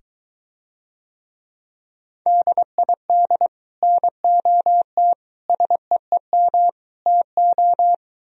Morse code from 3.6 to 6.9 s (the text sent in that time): NOT SEEM